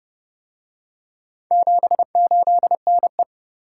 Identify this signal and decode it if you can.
Morse code, heard as a keyed tone, about 30 wpm, 700 Hz. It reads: 78NE